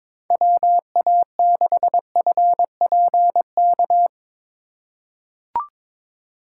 Morse code, heard as a keyed tone, about 21 wpm, 700 Hz. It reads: WA6FPK
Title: USA Call Signs - 6 Characters - 4x Character Spacing - 22wpm